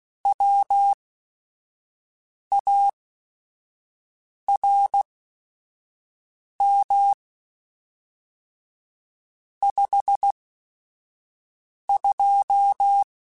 Morse code, heard as a keyed tone, about 16 wpm, spaced as for 5 wpm, 800 Hz. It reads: WARM 52
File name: Weather Sets / Weather Sets-mp3s / Weather 4 10 @5